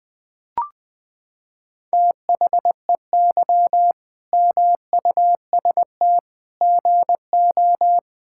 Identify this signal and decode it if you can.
Morse code, heard as a keyed tone, about 20 wpm, 700 Hz. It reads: THEY MUST GO
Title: Sentences from Top 1600 Words 20wpm